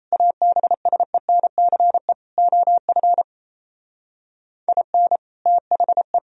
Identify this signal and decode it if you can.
Morse code, heard as a keyed tone, about 37 wpm, 700 Hz. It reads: ABSENCE OF IN THE